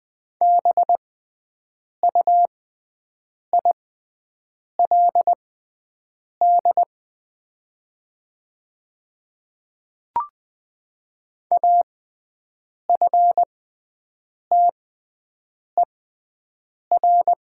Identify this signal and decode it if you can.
Morse code, heard as a keyed tone, about 20 wpm, spaced as for 7 wpm, 700 Hz. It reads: BUILD AFTER
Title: 6x Character Spacing - Top 200 Words - 20wpm